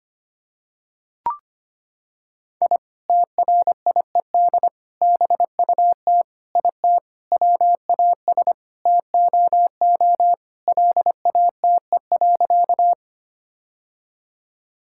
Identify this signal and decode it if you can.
Morse code, heard as a keyed tone, about 25 wpm, 700 Hz. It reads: I TRIED BUT IT WAS TOO LATE.
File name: Sentences with 3-Word Phrases - 25wpm